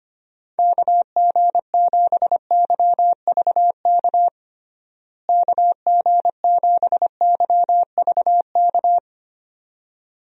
Morse code, heard as a keyed tone, about 25 wpm, 700 Hz. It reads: KG7YVK KG7YVK
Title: USA Call Signs - 6 Characters - Repeated 2x - 25wpm